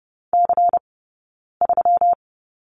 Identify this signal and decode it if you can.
Morse code, heard as a keyed tone, about 30 wpm, spaced as for 9 wpm, 700 Hz. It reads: C3
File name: Lesson 20d - Number 3 - Two Characters 30wpm